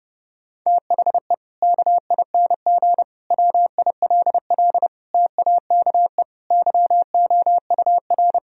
Morse code, heard as a keyed tone, about 30 wpm, 700 Hz. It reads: THE KING WILL TAKE YOUR
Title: Sentences from Top 1000 Words - Review Part 4 - 30wpm